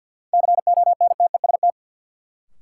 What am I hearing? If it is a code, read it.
Morse code, heard as a keyed tone, about 48 wpm, 700 Hz. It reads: CONTEST
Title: Sentences from Top 1600 Words - New Word Drill Alt - 50wpm